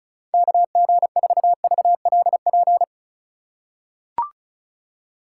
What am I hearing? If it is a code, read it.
Morse code, heard as a keyed tone, about 34 wpm, 700 Hz. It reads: KG4VLP
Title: USA Call Signs - 6 Characters - 4x Character Spacing - 35wpm